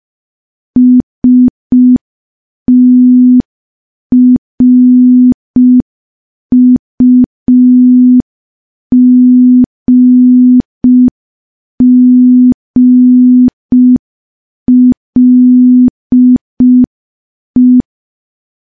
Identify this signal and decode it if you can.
Morse code, heard as a keyed tone, about 5 wpm, 250 Hz. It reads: STRUGGLE
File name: morse / morse-training-05WPM